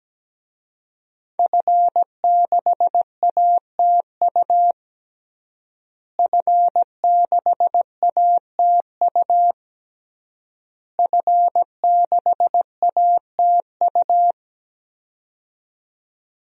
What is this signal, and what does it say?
Morse code, heard as a keyed tone, about 17 wpm, 700 Hz. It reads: F6ATU F6ATU F6ATU
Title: DXCC-100 Call Signs - 5 Characters - Repeated 3x - 17wpm